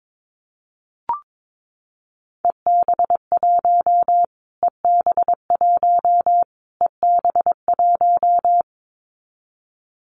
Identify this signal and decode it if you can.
Morse code, heard as a keyed tone, about 22 wpm, 700 Hz. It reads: EB1 EB1 EB1
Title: Callsign Prefixes for Top 25 DXCC - Repeated 3x - 22wpm